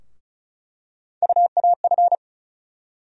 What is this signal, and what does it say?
Morse code, heard as a keyed tone, about 36 wpm, 700 Hz. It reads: UAF
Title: USA Call Sign Postfixes - 3 Letters 35wpm